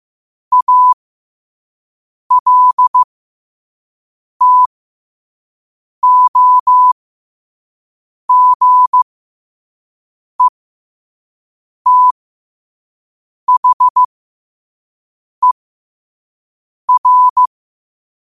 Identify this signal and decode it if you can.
Morse code, heard as a keyed tone, about 15 wpm, spaced as for 5 wpm, 1000 Hz. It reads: ALTOGETHER
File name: qotdfiles / 5wpm